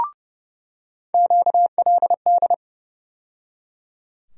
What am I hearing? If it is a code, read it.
Morse code, heard as a keyed tone, about 32 wpm, 700 Hz. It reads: QLD
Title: USA Call Sign Postfixes - 3 Letters 30wpm